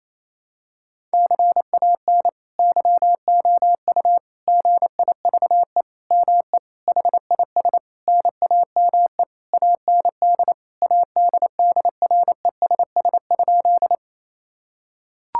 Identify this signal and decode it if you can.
Morse code, heard as a keyed tone, about 28 wpm, 700 Hz. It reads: CAN YOU GIVE ME HIS NAME AND ADDRESS?
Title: Sentences from Top 1000 Words - Review Part 6 - 28wpm